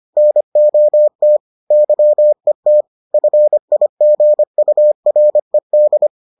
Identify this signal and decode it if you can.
Morse code, heard as a keyed tone, about 25 wpm, 600 Hz. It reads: NOT YET FIGURED